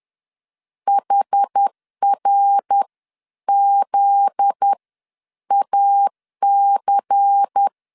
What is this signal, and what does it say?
Morse code, heard as a keyed tone, about 10 wpm, spaced as for 9 wpm, 800 Hz. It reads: HRZAC
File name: LEZIONI / 7 GF9Z / 7 GF9Z 10wpm 2611